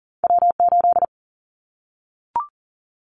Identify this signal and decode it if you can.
Morse code, heard as a keyed tone, about 42 wpm, 700 Hz. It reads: W8
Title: Callsign Prefixes for Top 25 DXCC - 40wpm